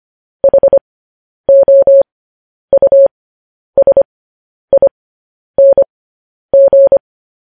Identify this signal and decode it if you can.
Morse code, heard as a keyed tone, about 25 wpm, spaced as for 10 wpm, 550 Hz. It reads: HOUSING
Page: news-headlines-in-morse-code-at-10-wpm